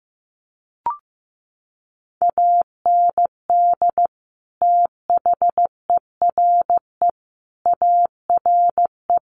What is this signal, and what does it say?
Morse code, heard as a keyed tone, about 15 wpm, 700 Hz. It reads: AND THERE ARE
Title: Sets of 5 Words Encore2 15wpm